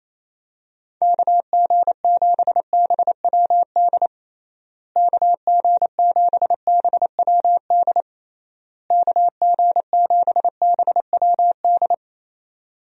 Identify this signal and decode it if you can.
Morse code, heard as a keyed tone, about 27 wpm, 700 Hz. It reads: KG7BWD KG7BWD KG7BWD
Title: USA Call Signs - 6 Characters - Repeated 3x - 28wpm